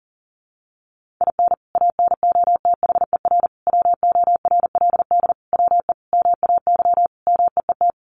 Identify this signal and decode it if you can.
Morse code, heard as a keyed tone, about 37 wpm, 700 Hz. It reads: IN ANOTHER WORLD WE MAY MEET
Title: Sentences from Top 1100 Words 40wpm